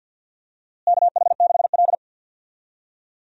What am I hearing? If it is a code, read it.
Morse code, heard as a keyed tone, about 51 wpm, 700 Hz. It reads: KH6L
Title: DXCC-100 Call Signs - 4 Characters - Repeated 3x - 50wpm